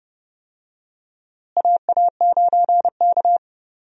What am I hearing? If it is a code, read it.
Morse code, heard as a keyed tone, about 30 wpm, 700 Hz. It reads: AA9K
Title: USA Call Signs - 4 Characters - Repeated 2x - 30wpm